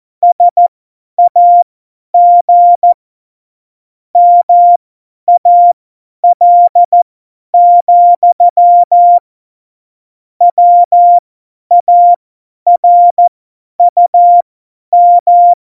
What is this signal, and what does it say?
Morse code, heard as a keyed tone, about 14 wpm, spaced as for 10 wpm, 700 Hz. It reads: SAG MAL, WARUM
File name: Morsekurs / Witze / Witz 07. November 2017, 50 F70